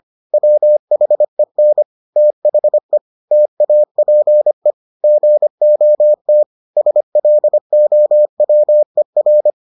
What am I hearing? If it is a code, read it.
Morse code, heard as a keyed tone, about 25 wpm, 600 Hz. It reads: WHEN THE TAPE GOT SLOWER